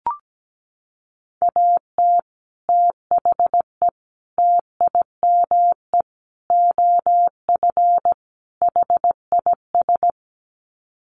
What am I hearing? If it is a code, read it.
Morse code, heard as a keyed tone, about 17 wpm, 700 Hz. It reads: AT THE TIME OF HIS
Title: Sets of 5 Words Encore2 17wpm